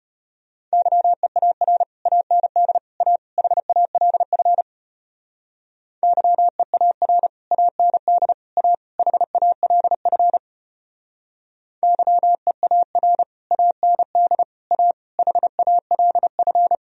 Morse code, heard as a keyed tone, about 30 wpm, 700 Hz. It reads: YEAR AND A HALF YEAR AND A HALF YEAR AND A HALF
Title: Speed-Racing - Sets of 4 Words 25wpm